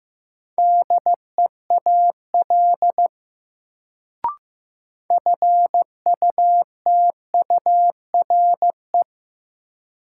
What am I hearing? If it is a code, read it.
Morse code, heard as a keyed tone, about 15 wpm, 700 Hz. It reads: DEAL FUTURE
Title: Sets of 3 Words - New Word Drill - 15wpm